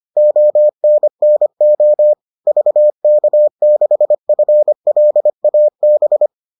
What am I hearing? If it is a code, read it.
Morse code, heard as a keyed tone, about 25 wpm, 600 Hz. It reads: ONNO VK6FLAB